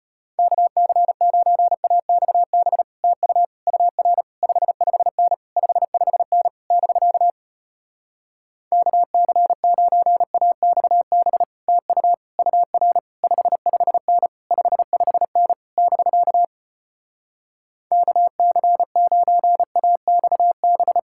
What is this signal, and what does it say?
Morse code, heard as a keyed tone, about 33 wpm, 700 Hz. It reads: KC9AXB TU UR 55N 55N <BK> KC9AXB TU UR 55N 55N <BK> KC9AXB